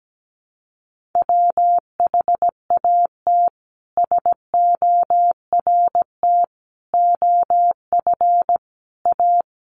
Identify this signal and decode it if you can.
Morse code, heard as a keyed tone, about 17 wpm, 700 Hz. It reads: WHAT SORT OF A
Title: Sentences from Top 1300 Words 17wpm